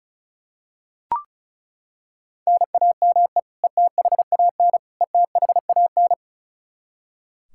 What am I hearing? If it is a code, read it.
Morse code, heard as a keyed tone, about 34 wpm, 700 Hz. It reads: NAME ETHAN ETHAN
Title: Practice QSO - Exchange 1 - Beginning - 35wpm